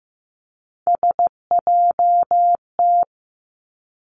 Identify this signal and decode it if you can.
Morse code, heard as a keyed tone, about 15 wpm, 700 Hz. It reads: SJT
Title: Three Challenging Letter-Number Combinations 15wpm